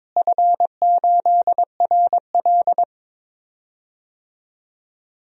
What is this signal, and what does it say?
Morse code, heard as a keyed tone, about 21 wpm, 700 Hz. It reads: F8RL
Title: DXCC-100 Call Signs - 4 Characters - Repeated 2x - 22wpm